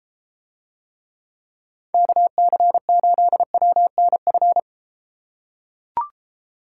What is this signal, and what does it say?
Morse code, heard as a keyed tone, about 33 wpm, 700 Hz. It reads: KC8WNF